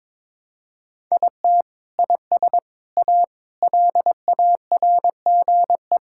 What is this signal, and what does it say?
Morse code, heard as a keyed tone, about 22 wpm, 700 Hz. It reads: IT IS A LARGE
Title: Sentences from Top 1000 Words - Review Part 5 - 22wpm